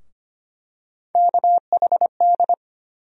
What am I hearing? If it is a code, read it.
Morse code, heard as a keyed tone, about 25 wpm, 700 Hz. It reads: KHD
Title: USA Call Sign Postfixes - 3 Letters 25wpm